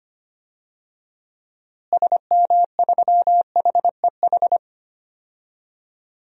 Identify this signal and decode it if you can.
Morse code, heard as a keyed tone, about 25 wpm, 700 Hz. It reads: SM3HEH
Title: DXCC-100 Call Signs - 6 Characters - Repeated 2x - 25wpm